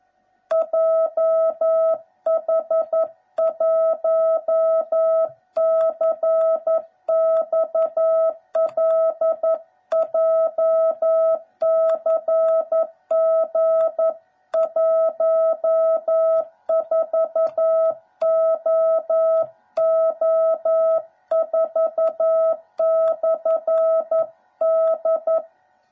Morse code, heard as a keyed tone, about 11 wpm, 660 Hz. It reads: JH1CXL JCG14OO4/D
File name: jh8cbh / 2021cw / cw09b